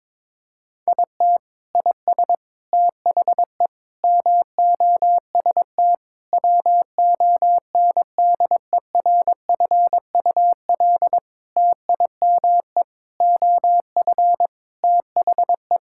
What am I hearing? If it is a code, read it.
Morse code, heard as a keyed tone, about 22 wpm, 700 Hz. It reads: IT IS THE MOST WONDERFUL TIME OF THE
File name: practice / 22wpm / Sentences with 8-Word Phrases - 22wpm